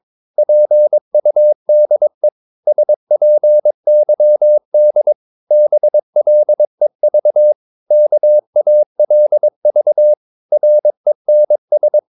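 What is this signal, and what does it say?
Morse code, heard as a keyed tone, about 22 wpm, 600 Hz. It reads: PUDE SPYD BLEV KALV RENS
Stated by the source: Morse practice of short Danish words at 22 wpm, from CW med Gnister.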